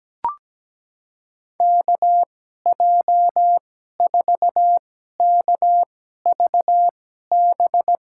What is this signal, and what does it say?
Morse code, heard as a keyed tone, about 17 wpm, spaced as for 12 wpm, 700 Hz. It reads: KJ4KVB
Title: USA Call Signs - 6 Characters - 2x Character Spacing - 17wpm